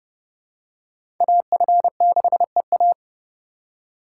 Morse code, heard as a keyed tone, about 30 wpm, 700 Hz. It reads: AF6EA
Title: USA Call Signs - 5 Characters - 4x Character Spacing - 30wpm